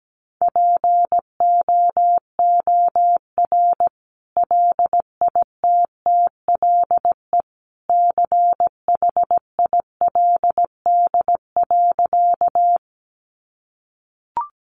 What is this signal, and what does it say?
Morse code, heard as a keyed tone, about 17 wpm, 700 Hz. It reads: POOR LITTLE CHILD.
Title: Sentences from Top 1000 Words 17wpm